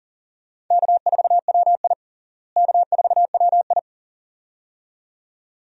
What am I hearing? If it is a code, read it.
Morse code, heard as a keyed tone, about 40 wpm, 700 Hz. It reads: K4WI K4WI
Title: USA Call Signs - 4 Characters - Repeated 2x - 40wpm